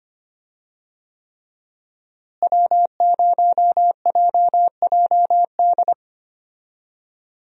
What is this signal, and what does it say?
Morse code, heard as a keyed tone, about 25 wpm, 700 Hz. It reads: W0JJD
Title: USA Call Signs - 5 Characters - Repeated 3x - 25wpm